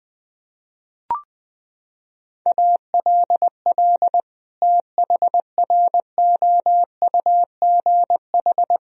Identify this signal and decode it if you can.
Morse code, heard as a keyed tone, about 20 wpm, 700 Hz. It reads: ALL THROUGH
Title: Sentences from Top 1000 Words - Review Part 1 - 20wpm